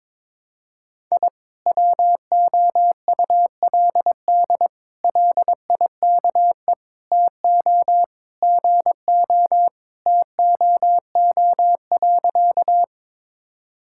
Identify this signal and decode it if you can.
Morse code, heard as a keyed tone, about 21 wpm, 700 Hz. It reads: I WOULD LIKE TO GO TOO.
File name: practice / 22wpm / Sentences with 3-Word Phrases - 22wpm